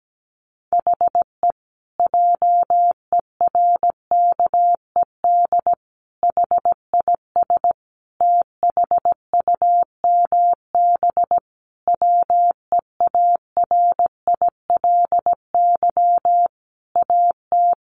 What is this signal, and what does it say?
Morse code, heard as a keyed tone, about 17 wpm, 700 Hz. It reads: HE JERKED HIS THUMB WEARILY AT
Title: A Sound Of Thunder 17wpm